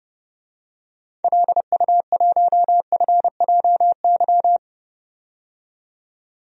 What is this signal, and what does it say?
Morse code, heard as a keyed tone, about 30 wpm, 700 Hz. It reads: LU1FJY
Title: DXCC-100 Call Signs - 6 Characters - 30wpm